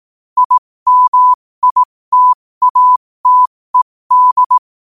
Morse code, heard as a keyed tone, about 18 wpm, spaced as for 15 wpm, 1000 Hz. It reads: IMITATED
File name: qotdfiles / 15wpm